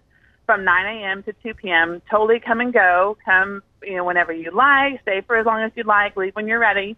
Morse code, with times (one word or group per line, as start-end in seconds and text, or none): none